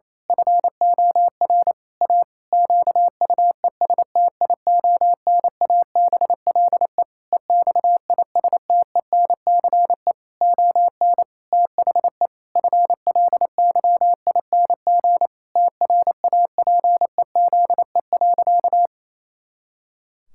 Morse code, heard as a keyed tone, about 27 wpm, 700 Hz. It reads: FOR A QUESTIONABLE EXISTENCE ON THE FLYING TRAPEZE.
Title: Pangram Challenge v2 - 28wpm